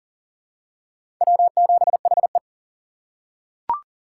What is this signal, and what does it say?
Morse code, heard as a keyed tone, about 39 wpm, 700 Hz. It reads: W7HE
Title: USA Call Signs - 4 Characters - 4x Character Spacing - 40wpm